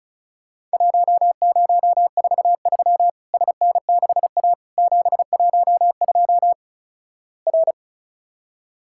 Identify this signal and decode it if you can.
Morse code, heard as a keyed tone, about 34 wpm, 700 Hz. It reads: 1043 SN6A 712 R
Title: Worked All Europe DX Contest - QTC Exchange - 35wpm